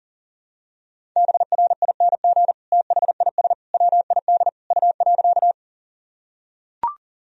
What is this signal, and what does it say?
Morse code, heard as a keyed tone, about 40 wpm, 700 Hz. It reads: BRING THIS WID U.
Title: Sentences from Top 500 Words and Abbreviations 40wpm